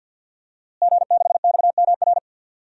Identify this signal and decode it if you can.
Morse code, heard as a keyed tone, about 47 wpm, 700 Hz. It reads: G6XMR
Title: DXCC-100 Call Signs - 5 Characters - Repeated 2x - 50wpm